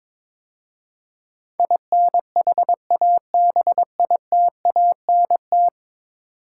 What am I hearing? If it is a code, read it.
Morse code, heard as a keyed tone, about 22 wpm, 700 Hz. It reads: INHABITANT
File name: practice / 22wpm / 10 Letter Words - First 3 Letters Given - 22wpm